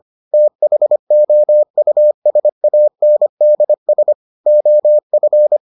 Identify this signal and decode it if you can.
Morse code, heard as a keyed tone, about 25 wpm, 600 Hz. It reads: THOUSANDS OF